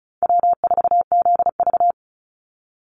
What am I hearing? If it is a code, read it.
Morse code, heard as a keyed tone, about 34 wpm, 700 Hz. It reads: W4ZV
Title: NA SOTA Chaser Callsigns 35wpm